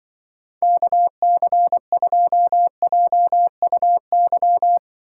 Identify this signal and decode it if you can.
Morse code, heard as a keyed tone, about 23 wpm, 700 Hz. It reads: KC2JUY
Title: Speed Racing - USA 2x3 Call Signs 20wpm